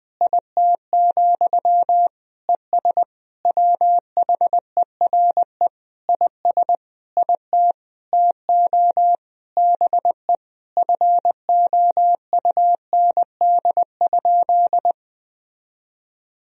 Morse code, heard as a keyed tone, about 20 wpm, 700 Hz. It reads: IT, ES WHERE IS IT TO BE FOUND?
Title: Sentences from Top 300 Words and Abbreviations 20wpm